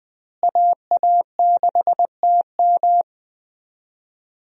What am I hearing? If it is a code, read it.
Morse code, heard as a keyed tone, about 20 wpm, 700 Hz. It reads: AA6TM
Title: USA Call Signs - 5 Characters - Repeated 3x - 20wpm